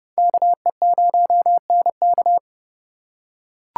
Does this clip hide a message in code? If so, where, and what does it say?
Yes: Morse code from 0.2 to 2.4 s, heard as a keyed tone, about 28 wpm, 700 Hz. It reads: KE0NK